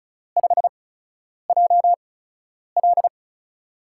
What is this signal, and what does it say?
Morse code, heard as a keyed tone, about 36 wpm, spaced as for 10 wpm, 700 Hz. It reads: 5JL
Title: USA Call Signs - 8x Character Spacing - 35wpm